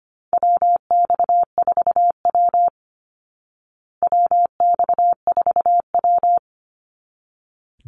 Morse code, heard as a keyed tone, about 25 wpm, 700 Hz. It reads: WX4W WX4W
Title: Call Signs Repeated 3x - 25wpm